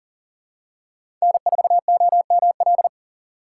Morse code, heard as a keyed tone, about 40 wpm, 700 Hz. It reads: N4OML
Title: USA Call Signs - 5 Characters - 2x Character Spacing - 40wpm